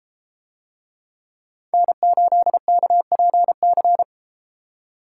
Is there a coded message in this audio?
Yes: Morse code from 1.7 to 4.0 s, heard as a keyed tone, about 32 wpm, 700 Hz. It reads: N8KPC